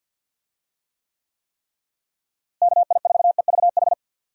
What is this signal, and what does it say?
Morse code, heard as a keyed tone, about 51 wpm, 700 Hz. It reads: KI4EVH
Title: USA Call Signs - 6 Characters - Repeated 3x - 50wpm